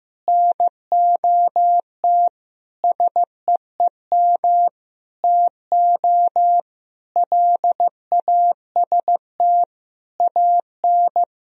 Morse code, heard as a keyed tone, about 15 wpm, 700 Hz. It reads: NOT SEEM TO LAST AN
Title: Sentences from Top 1000 Words - Review Part 1 - 15wpm